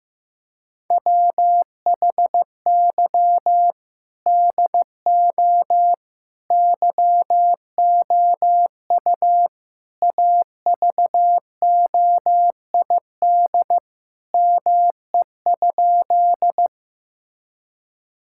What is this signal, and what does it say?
Morse code, heard as a keyed tone, about 15 wpm, 700 Hz. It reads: WHY DO YOU AVOID ME?
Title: Sentences from Top 1000 Words - Review Part 6 - 15wpm